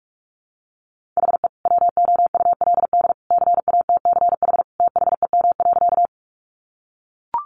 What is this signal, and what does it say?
Morse code, heard as a keyed tone, about 44 wpm, 700 Hz. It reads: HE WOULD CATCH THEM.
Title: Sentences from Top 1500 Words 45wpm